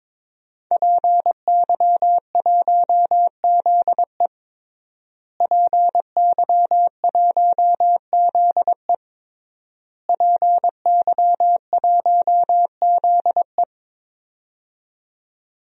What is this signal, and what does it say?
Morse code, heard as a keyed tone, about 22 wpm, 700 Hz. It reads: PY1ZE PY1ZE PY1ZE
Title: DXCC-100 Call Signs - 5 Characters - Repeated 3x - 22wpm